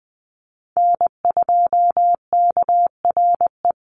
Morse code, heard as a keyed tone, about 20 wpm, 700 Hz. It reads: N2KRE
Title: USA Call Signs Repeated 2x 20wpm